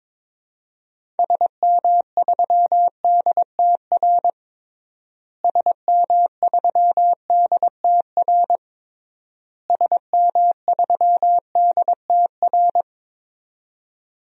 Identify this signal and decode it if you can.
Morse code, heard as a keyed tone, about 22 wpm, 700 Hz. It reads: SM3DTR SM3DTR SM3DTR
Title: DXCC-100 Call Signs - 6 Characters - Repeated 3x - 22wpm